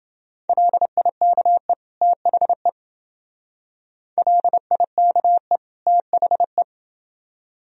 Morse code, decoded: LIKE THE LIKE THE

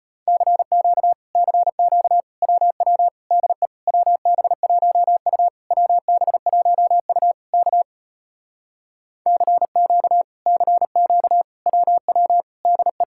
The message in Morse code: CQ CQ WW DE WB1U WB1U K CQ CQ WW DE

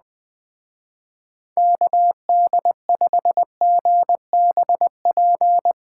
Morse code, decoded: KD5GBP

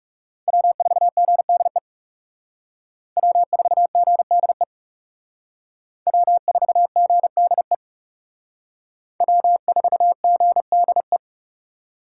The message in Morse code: W4GDE W4GDE W4GDE W4GDE

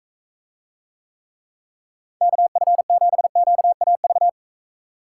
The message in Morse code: KF7QAV

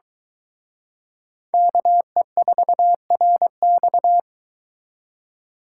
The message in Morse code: KE4RX

